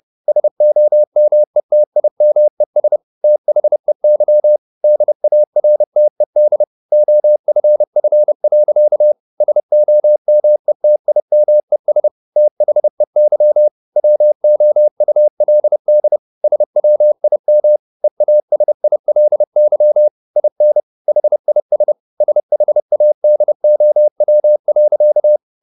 Morse code, heard as a keyed tone, about 30 wpm, 600 Hz. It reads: SOMETIMES THEY DARTED OFF. SOMETIMES THEY WOULD SWIM EASILY IN HIS SHADOW.